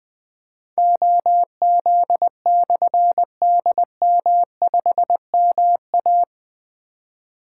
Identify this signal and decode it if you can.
Morse code, heard as a keyed tone, about 20 wpm, 700 Hz. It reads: OZ/DM5MA
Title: Difficult Call Signs 20wpm